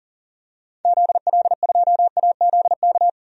Morse code, heard as a keyed tone, about 40 wpm, 700 Hz. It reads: ZL2AZK
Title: DXCC-100 Call Signs - 6 Characters - 40wpm